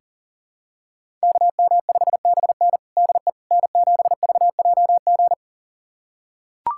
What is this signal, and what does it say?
Morse code, heard as a keyed tone, about 40 wpm, 700 Hz. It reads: KM5BN DE N7VJG
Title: USA Callsign DE Callsign - 5 Characters - 40wpm